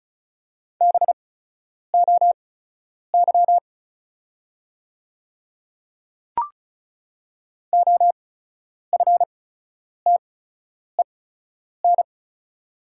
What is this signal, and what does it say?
Morse code, heard as a keyed tone, about 36 wpm, spaced as for 10 wpm, 700 Hz. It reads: BOY OFTEN
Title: 8x Character Spacing - Top 300 Words - 35wpm